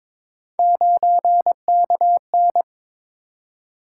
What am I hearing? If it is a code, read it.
Morse code, heard as a keyed tone, about 22 wpm, 700 Hz. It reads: 9KN
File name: practice / 22wpm / Three Letter-Number Combinations - 22wpm